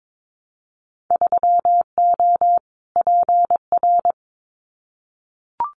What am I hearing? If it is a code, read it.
Morse code, heard as a keyed tone, about 22 wpm, 700 Hz. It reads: 3O PR
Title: Winter Field Day Exchanges Encore 22wpm